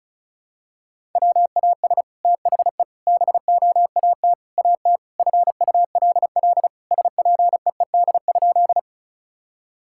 Morse code, heard as a keyed tone, about 34 wpm, 700 Hz. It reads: WAS THE BOAT AT FULL SPEED?
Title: Sentences from Top 1000 Words - Review Part 5 - 35wpm